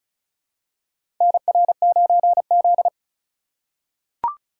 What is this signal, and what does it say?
Morse code, heard as a keyed tone, about 34 wpm, 700 Hz. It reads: NR9Z
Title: USA Call Signs - 4x Character Spacing - 35wpm